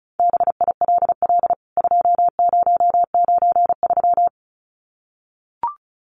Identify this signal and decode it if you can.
Morse code, heard as a keyed tone, about 34 wpm, 700 Hz. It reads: BILL 2093
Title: CWT Exchange 35wpm